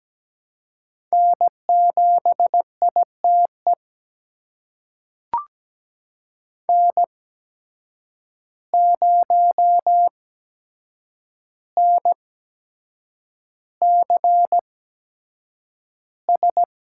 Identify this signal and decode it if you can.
Morse code, heard as a keyed tone, about 17 wpm, spaced as for 7 wpm, 700 Hz. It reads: N7ITE N0NCS
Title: USA Call Signs - 5 Characters - 8x Character Spacing - 17wpm